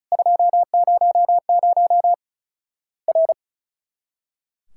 Morse code, heard as a keyed tone, about 36 wpm, 700 Hz. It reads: 200 R
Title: Worked All Europe DX Contest - QTC Exchange - 35wpm